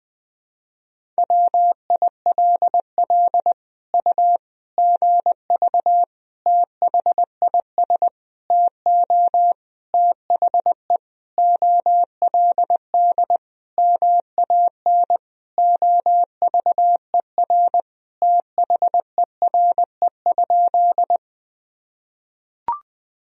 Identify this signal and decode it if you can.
Morse code, heard as a keyed tone, about 20 wpm, 700 Hz. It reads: WILL U GV THIS TO THE OLD MAN OVER THERE?
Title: Sentences from Top 200 Words and Abbreviations 20wpm